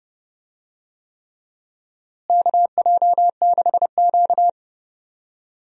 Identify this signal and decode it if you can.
Morse code, heard as a keyed tone, about 30 wpm, 700 Hz. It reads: KJ6Q
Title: USA Call Signs - 4 Characters - Repeated 3x - 30wpm